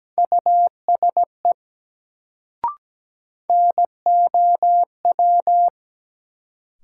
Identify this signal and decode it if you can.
Morse code, heard as a keyed tone, about 17 wpm, 700 Hz. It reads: USE NOW
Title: Sentences from Top 100 Words - New Word Drill - 17wpm